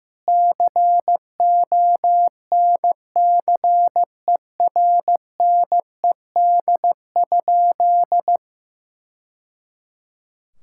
Morse code, heard as a keyed tone, about 16 wpm, 700 Hz. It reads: CONCERNED?
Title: Sentences from Top 1000 Words - Review Part 6 - 15wpm